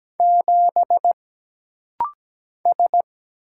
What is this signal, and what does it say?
Morse code, heard as a keyed tone, about 17 wpm, 700 Hz. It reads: 7 S
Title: Single Letter-Number - Rapid-Fire 17wpm